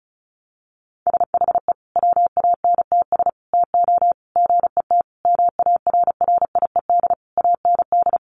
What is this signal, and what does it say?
Morse code, heard as a keyed tone, about 34 wpm, 700 Hz. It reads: SHE WANTS TO GET MARRIED AND
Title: Easy Phrases 35wpm